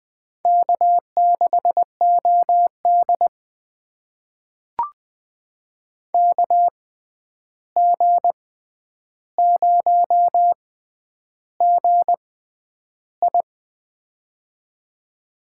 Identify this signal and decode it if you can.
Morse code, heard as a keyed tone, about 21 wpm, spaced as for 10 wpm, 700 Hz. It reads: K6OD KG0GI
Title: USA Call Signs - 6x Character Spacing - 20wpm